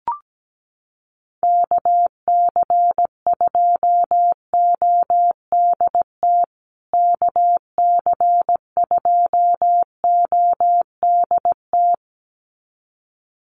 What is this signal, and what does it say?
Morse code, heard as a keyed tone, about 17 wpm, 700 Hz. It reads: KC2ODT KC2ODT
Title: USA Call Signs Repeated 2x 17wpm